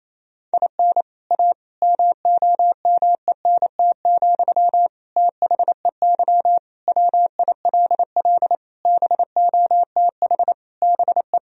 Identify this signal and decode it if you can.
Morse code, heard as a keyed tone, about 28 wpm, 700 Hz. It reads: IN A MOMENT, THEY WILL BOTH BE
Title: Sentences from Top 1000 Words - Review Part 2 - 28wpm